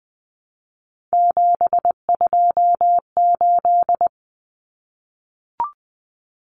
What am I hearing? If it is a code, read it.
Morse code, heard as a keyed tone, about 20 wpm, 700 Hz. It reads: 728
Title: Common Cut Numbers - 3-Digit 20wpm